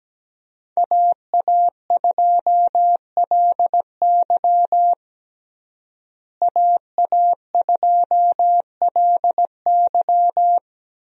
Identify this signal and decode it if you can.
Morse code, heard as a keyed tone, about 17 wpm, 700 Hz. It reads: AA2LY AA2LY
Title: USA Call Signs - 5 Characters - Repeated 3x - 17wpm